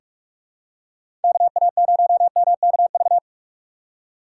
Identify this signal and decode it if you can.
Morse code, heard as a keyed tone, about 46 wpm, 700 Hz. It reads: KA0MKV
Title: USA Call Signs - 6 Characters - 2x Character Spacing - 45wpm